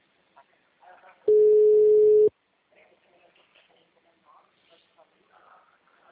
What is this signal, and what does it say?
Ringback tone (Europe, 425 Hz), with speech.